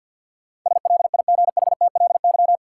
Morse code, heard as a keyed tone, about 49 wpm, 700 Hz. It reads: SLIGHTLY
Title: Sentences from Top 1300 Words - New Word Drill Alt - 50wpm